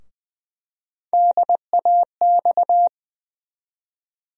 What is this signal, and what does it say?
Morse code, heard as a keyed tone, about 20 wpm, 700 Hz. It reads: DAX
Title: USA Call Sign Postfixes - 3 Letters 20wpm